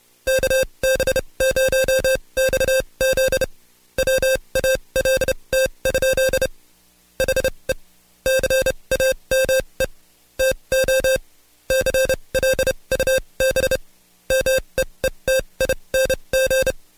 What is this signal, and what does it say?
Morse code, heard as a keyed tone, about 29 wpm, spaced as for 23 wpm, 530 Hz. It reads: KB0XZ WALT? HE CAME TO CLUB MEETING
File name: Code Quick / AudioDownld2 / Seg138-Stepper11 23WPM